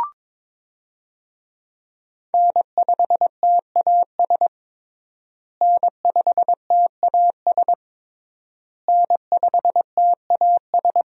Morse code, heard as a keyed tone, about 23 wpm, 700 Hz. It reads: N5TAS N5TAS N5TAS
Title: USA Call Signs - 5 Characters - Repeated 3x - 22wpm